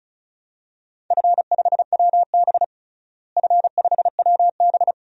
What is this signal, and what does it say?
Morse code, heard as a keyed tone, about 34 wpm, 700 Hz. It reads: F5WB F5WB
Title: DXCC-100 Call Signs - 4 Characters - Repeated 2x - 35wpm